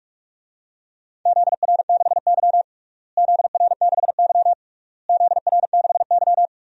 Morse code, heard as a keyed tone, about 43 wpm, 690 Hz. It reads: ZR6Y ZR6Y ZR6Y
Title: DXCC-100 Call Signs - 4 Characters - Repeated 3x - 45wpm